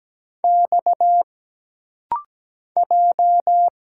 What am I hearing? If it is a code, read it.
Morse code, heard as a keyed tone, about 17 wpm, 700 Hz. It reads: X J
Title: Single Letters - Rapid-Fire 17wpm